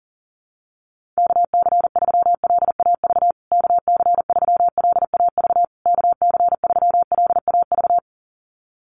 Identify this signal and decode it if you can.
Morse code, heard as a keyed tone, about 40 wpm, 700 Hz. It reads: KC3LAV KC3LAV KC3LAV
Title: USA Call Signs Repeated 3x 40wpm